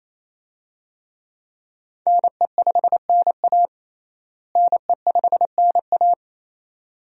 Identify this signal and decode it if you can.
Morse code, heard as a keyed tone, about 28 wpm, 700 Hz. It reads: NE5NA NE5NA
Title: USA Call Signs - 5 Characters - Repeated 2x - 28wpm